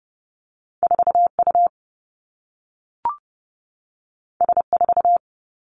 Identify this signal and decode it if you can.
Morse code, heard as a keyed tone, about 30 wpm, 700 Hz. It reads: 4 U S 4
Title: Two Challenging Letter-Number Combinations 30wpm